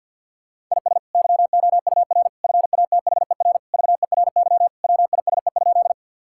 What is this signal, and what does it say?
Morse code, heard as a keyed tone, about 47 wpm, 700 Hz. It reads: IS YOUR FATHER VERY WISE?
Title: Sentences from Top 1600 Words 50wpm